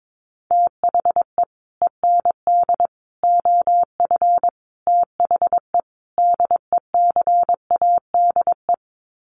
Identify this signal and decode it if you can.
Morse code, heard as a keyed tone, about 22 wpm, 700 Hz. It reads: THE END OF THE DECADE
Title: Sets of 5 Words Encore 22wpm